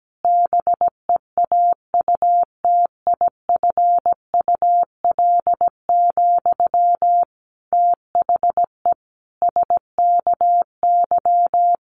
Morse code, heard as a keyed tone, about 17 wpm, 700 Hz. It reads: BEAUTIFUL, THE SKY
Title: Sentences from Top 1200 Words 17wpm